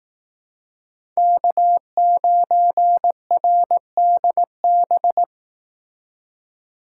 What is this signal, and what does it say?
Morse code, heard as a keyed tone, about 18 wpm, 700 Hz. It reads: K9RDB
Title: Speed Racing - USA Call Signs 15wpm